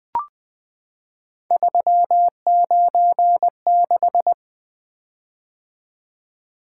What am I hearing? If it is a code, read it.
Morse code, heard as a keyed tone, about 20 wpm, 700 Hz. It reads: 396
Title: Three Letter-Number Combinations - 20wpm